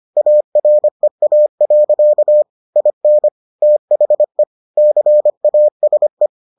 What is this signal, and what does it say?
Morse code, heard as a keyed tone, about 25 wpm, 600 Hz. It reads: AREA. IN THE CASE